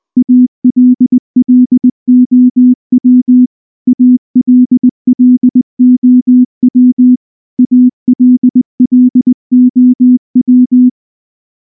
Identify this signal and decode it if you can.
Morse code, heard as a keyed tone, about 19 wpm, 250 Hz. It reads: ALLOW ALLOW ALLOW
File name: morse / morse-training-20WPM-10